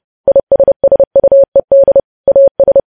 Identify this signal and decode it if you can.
Morse code, heard as a keyed tone, about 30 wpm, 550 Hz. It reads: ISSUED AS
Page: news headlines in morse code at 30 wpm